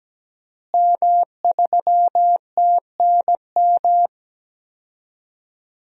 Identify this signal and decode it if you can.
Morse code, heard as a keyed tone, about 17 wpm, 700 Hz. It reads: M3TNM
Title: DXCC-100 Call Signs - 5 Characters - Repeated 2x - 17wpm